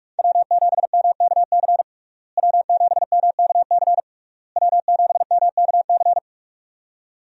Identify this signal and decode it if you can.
Morse code, heard as a keyed tone, about 46 wpm, 700 Hz. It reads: W7MKC W7MKC W7MKC
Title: USA Call Signs - 5 Characters - Repeated 3x - 45wpm